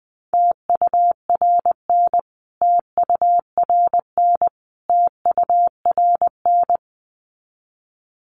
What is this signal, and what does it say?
Morse code, heard as a keyed tone, about 20 wpm, 700 Hz. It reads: TURN TURN TURN
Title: Top 500 Words Repeated 20wpm